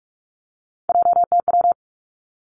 Morse code, heard as a keyed tone, about 46 wpm, 700 Hz. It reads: JTW
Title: Three Challenging Letter-Number Combinations 45wpm